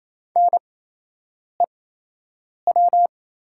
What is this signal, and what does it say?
Morse code, heard as a keyed tone, about 28 wpm, spaced as for 8 wpm, 690 Hz. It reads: NEW
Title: 8x Character Spacing - Top 200 Words - 28wpm